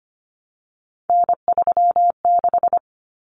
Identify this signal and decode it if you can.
Morse code, heard as a keyed tone, about 25 wpm, 700 Hz. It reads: N36
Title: Common Cut Numbers - 3-Digit 25wpm